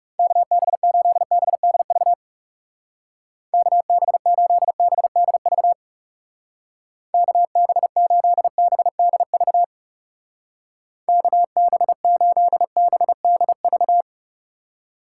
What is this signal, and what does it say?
Morse code, heard as a keyed tone, about 37 wpm, 700 Hz. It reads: KB8BDV KB8BDV KB8BDV KB8BDV
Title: Speed Racing - USA 2x3 Call Signs 30wpm